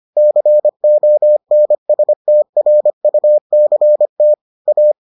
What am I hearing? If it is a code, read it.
Morse code, heard as a keyed tone, about 25 wpm, 600 Hz. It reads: CONSTRUCT A